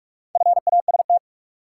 Morse code, heard as a keyed tone, about 46 wpm, 700 Hz. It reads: FAST
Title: Sentences from Top 400 Words - New Word Drill - 45wpm